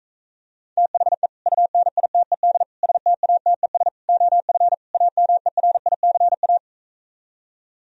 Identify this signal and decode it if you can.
Morse code, heard as a keyed tone, about 45 wpm, 700 Hz. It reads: THE UNITED STATES OF AMERICA